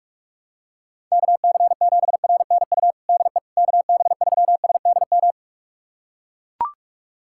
Morse code, heard as a keyed tone, about 41 wpm, 700 Hz. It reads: KC7RNU DE KB3SDM
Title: USA Callsign DE Callsign - 6 Characters - 45wpm